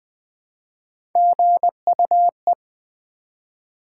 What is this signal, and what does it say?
Morse code, heard as a keyed tone, about 20 wpm, 700 Hz. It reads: GUE
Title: Build A Word - 1 Letter At A Time - 5-Letter Words - 20wpm